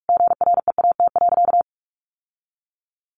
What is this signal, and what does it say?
Morse code, heard as a keyed tone, about 44 wpm, 700 Hz. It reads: GREAT.
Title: Sentences from Top 1200 Words 45wpm